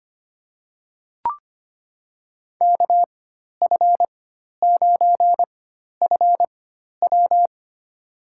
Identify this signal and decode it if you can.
Morse code, heard as a keyed tone, about 25 wpm, spaced as for 12 wpm, 700 Hz. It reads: KF9FW
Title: USA Call Signs - 4x Character Spacing - 25wpm